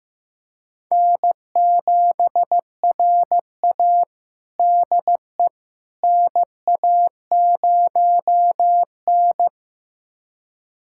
Morse code, heard as a keyed tone, about 15 wpm, 700 Hz. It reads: N7RA DE NA0N
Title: USA Callsign DE Callsign - 4 Characters - 15wpm